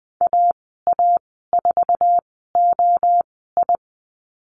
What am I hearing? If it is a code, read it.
Morse code, heard as a keyed tone, about 20 wpm, spaced as for 14 wpm, 700 Hz. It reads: AA4OI
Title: 055 Lesson 10g - Letter r - Call Signs 20wpm